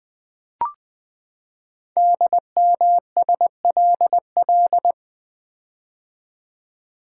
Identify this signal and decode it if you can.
Morse code, heard as a keyed tone, about 20 wpm, 700 Hz. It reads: DMSLL
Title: Code Groups - 5 Letter:Numbers - 20wpm